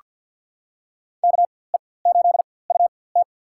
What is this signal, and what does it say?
Morse code, heard as a keyed tone, about 46 wpm, spaced as for 23 wpm, 700 Hz. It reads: KE8UT
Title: USA Call Signs - 5 Characters - 4x Character Spacing - 50wpm